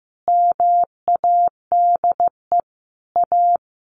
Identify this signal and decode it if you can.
Morse code, heard as a keyed tone, about 15 wpm, 700 Hz. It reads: MADE A
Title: Sentences from Top 1400 Words 15wpm